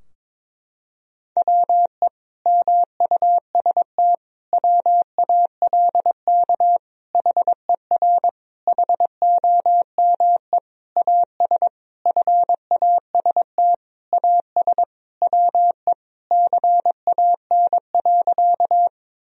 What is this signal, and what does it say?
Morse code, heard as a keyed tone, about 22 wpm, 700 Hz. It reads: WE MUST WALK HER HOME AS FAST AS WE CAN.
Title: Sentences from Top 1000 Words - Review Part 2 - 22wpm